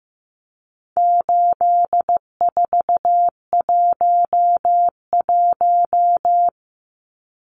Morse code, heard as a keyed tone, about 15 wpm, 700 Hz. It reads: 8411
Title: Numbers - 4-Digit 15wpm